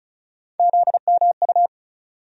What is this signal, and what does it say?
Morse code, heard as a keyed tone, about 34 wpm, 700 Hz. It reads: ZMU
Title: USA Call Sign Postfixes - 3 Letters 35wpm